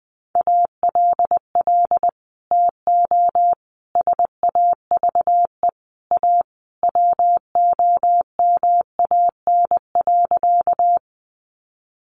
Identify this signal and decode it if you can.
Morse code, heard as a keyed tone, about 20 wpm, 700 Hz. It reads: ALL TO SAVE A WOMAN.
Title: Sentences from Top 1100 Words 20wpm